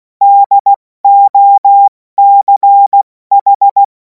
Morse code, heard as a keyed tone, about 16 wpm, 800 Hz. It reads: DOCH
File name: Morsekurs / Witze / Witz 02. Juni 2026 70 F80